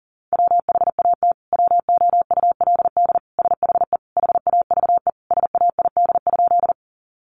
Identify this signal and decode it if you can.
Morse code, heard as a keyed tone, about 40 wpm, 700 Hz. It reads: WHAT WOULD SHE HAVE SAID?
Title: Sentences from Top 100 Words 40wpm